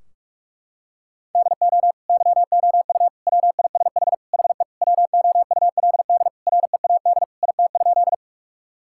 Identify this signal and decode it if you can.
Morse code, heard as a keyed tone, about 47 wpm, 700 Hz. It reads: DO YOU WISH HE WOULD READ IT?
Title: Sentences from Top 1600 Words 45wpm